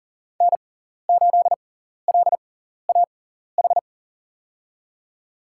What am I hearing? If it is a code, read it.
Morse code, heard as a keyed tone, about 40 wpm, spaced as for 14 wpm, 700 Hz. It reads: N8LAH
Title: USA Call Signs - 5 Characters - 6x Character Spacing - 40wpm